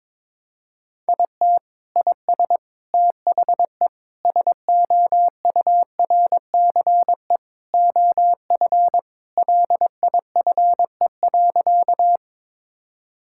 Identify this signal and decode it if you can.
Morse code, heard as a keyed tone, about 22 wpm, 700 Hz. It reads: IT IS THE SOURCE OF LIFE.
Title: Sentences from Top 1000 Words - Review Part 5 - 22wpm